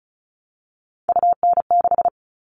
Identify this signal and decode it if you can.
Morse code, heard as a keyed tone, about 34 wpm, 700 Hz. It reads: UN6
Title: All Cut Numbers - 3-Digit 35wpm